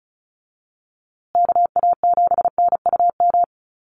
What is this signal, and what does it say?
Morse code, heard as a keyed tone, about 34 wpm, 700 Hz. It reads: KA7NUM
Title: USA Call Signs Repeated 2x 35wpm